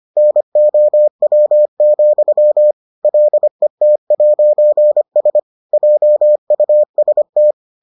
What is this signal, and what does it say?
Morse code, heard as a keyed tone, about 25 wpm, 600 Hz. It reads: NOW, LET'S JUST